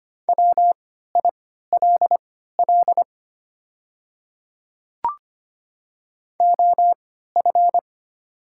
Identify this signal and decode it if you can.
Morse code, heard as a keyed tone, about 25 wpm, spaced as for 14 wpm, 700 Hz. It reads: WILL OF
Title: 3x Character Spacing - Top 100 Words - 25wpm